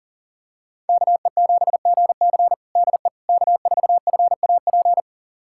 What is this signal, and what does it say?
Morse code, heard as a keyed tone, about 40 wpm, 700 Hz. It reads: KE7GC DE K4FAP
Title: USA Callsign DE Callsign 40wpm